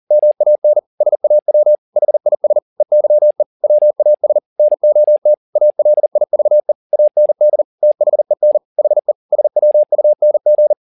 Morse code, heard as a keyed tone, about 40 wpm, 600 Hz. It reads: MAN SAW HIS EYE WAS NOT ALIVE AND THEN HE SWUNG
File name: the old man and the sea 2 40wpm